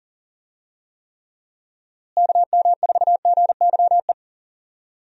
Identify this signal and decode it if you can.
Morse code, heard as a keyed tone, about 38 wpm, 700 Hz. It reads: KM4GYE
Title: USA Call Signs - 6 Characters - Repeated 3x - 40wpm